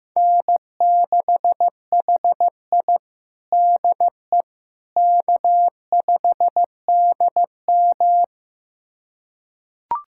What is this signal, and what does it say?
Morse code, heard as a keyed tone, about 15 wpm, 700 Hz. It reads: N6HI DE K5DM
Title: USA Callsign DE Callsign - 4 Characters - 15wpm